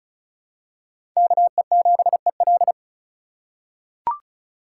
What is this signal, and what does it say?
Morse code, heard as a keyed tone, about 34 wpm, 700 Hz. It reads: KE7EL
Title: USA Call Signs - 5 Characters - 4x Character Spacing - 35wpm